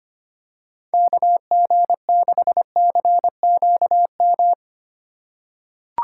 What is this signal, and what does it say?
Morse code, heard as a keyed tone, about 25 wpm, 700 Hz. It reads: KG6CQM
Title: USA Call Signs - 6 Characters - 6x Character Spacing - 25wpm